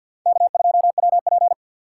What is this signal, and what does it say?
Morse code, heard as a keyed tone, about 53 wpm, 700 Hz. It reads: K2WP
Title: USA Call Signs - 4 Characters - 6x Character Spacing - 50wpm